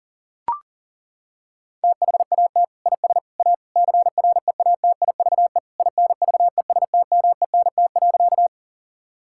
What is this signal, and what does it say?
Morse code, heard as a keyed tone, about 40 wpm, 700 Hz. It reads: THAT IS A CREATIVE INVESTMENT.
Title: Sentences from Top 1000 Words - Review Part 5 - 40wpm